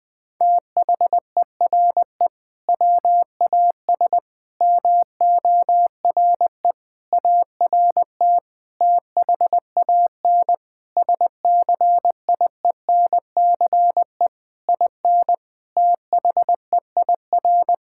Morse code, heard as a keyed tone, about 21 wpm, 700 Hz. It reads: THERE WAS MORE ART THAN SCIENCE IN THEIR
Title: Sentences from Top 1000 Words - Review Part 6 - 20wpm